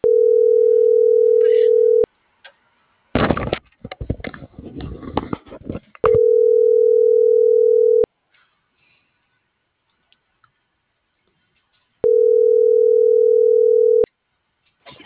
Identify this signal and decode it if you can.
Ringback tone (North America), 440 + 480 Hz.